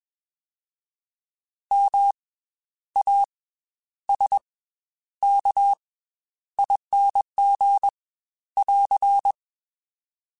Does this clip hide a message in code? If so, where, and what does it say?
Yes: Morse code from 1.7 to 9.3 s, heard as a keyed tone, about 21 wpm, 800 Hz. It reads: M A S K ING <AR>